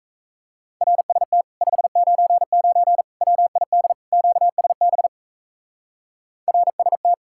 Morse code, heard as a keyed tone, about 38 wpm, 700 Hz. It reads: RST 599 WID QSB RST